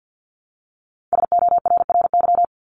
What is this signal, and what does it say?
Morse code, heard as a keyed tone, about 50 wpm, 700 Hz. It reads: SORRY
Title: Top 1751-2000 Words 50wpm